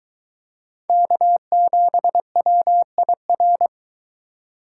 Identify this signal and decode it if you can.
Morse code, heard as a keyed tone, about 22 wpm, 700 Hz. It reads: K7WIR